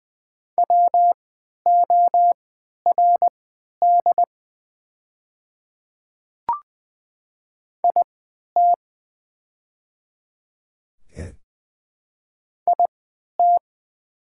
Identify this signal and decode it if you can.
Morse code, heard as a keyed tone, about 20 wpm, spaced as for 11 wpm, 700 Hz. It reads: WORD IT IT
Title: 3x Character Spacing - Top 100 Words - 20wpm